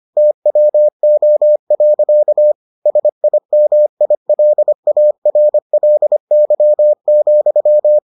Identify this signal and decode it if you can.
Morse code, heard as a keyed tone, about 25 wpm, 600 Hz. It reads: TWO. SIMILARLY,